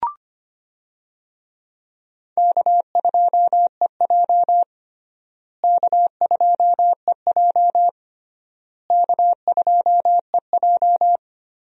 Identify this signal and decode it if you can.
Morse code, heard as a keyed tone, about 25 wpm, 700 Hz. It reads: K2EJ K2EJ K2EJ
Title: USA Call Signs - 4 Characters - Repeated 3x - 25wpm